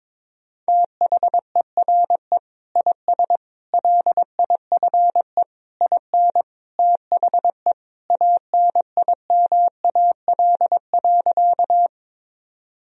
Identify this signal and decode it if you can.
Morse code, heard as a keyed tone, about 22 wpm, 700 Hz. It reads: THERE IS LIFE IN THE ANIMAL.
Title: Sentences from Top 1000 Words - Review Part 1 - 22wpm